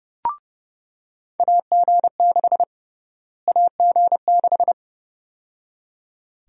Morse code, heard as a keyed tone, about 30 wpm, 700 Hz. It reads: AG6 AG6
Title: USA Call Sign Prefixes - Repeated 2x - 30wpm